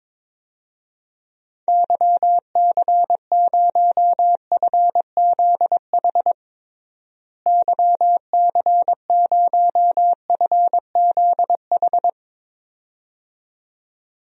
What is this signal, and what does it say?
Morse code, heard as a keyed tone, about 21 wpm, 700 Hz. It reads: YC0FZH YC0FZH
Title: DXCC-100 Call Signs - 6 Characters - Repeated 2x - 22wpm